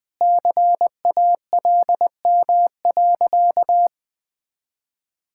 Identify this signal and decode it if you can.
Morse code, heard as a keyed tone, about 20 wpm, 700 Hz. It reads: CALM.
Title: Sentences with 7-Word Phrases - 20wpm